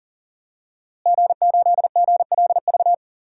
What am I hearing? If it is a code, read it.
Morse code, heard as a keyed tone, about 37 wpm, 700 Hz. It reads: G8GLV
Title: DXCC-100 Call Signs - 5 Characters - Repeated 3x - 40wpm